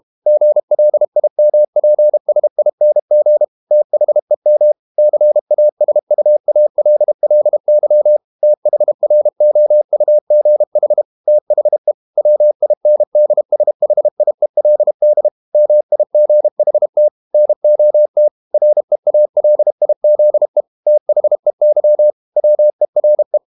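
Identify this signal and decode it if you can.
Morse code, heard as a keyed tone, about 32 wpm, 600 Hz. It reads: GLIMPSING THEM CASUALLY THROUGH THE WINDSHIELD MIGHT NOT REALIZE THEY WERE
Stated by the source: Morse code practice at 32 wpm from CW med Gnister, reading Ken Follett's novel Never in English.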